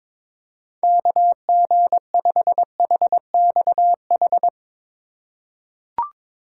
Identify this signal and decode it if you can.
Morse code, heard as a keyed tone, about 22 wpm, 700 Hz. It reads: KG5HXH
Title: USA Call Signs - 6 Characters - 4x Character Spacing - 22wpm